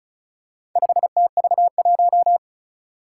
Morse code, heard as a keyed tone, about 36 wpm, 700 Hz. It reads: 5TV1
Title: Four Challenging Letter-Number Combinations 35wpm